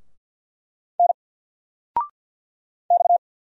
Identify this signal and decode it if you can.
Morse code, heard as a keyed tone, about 50 wpm, 700 Hz. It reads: N X
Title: Single Letter-Number 50wpm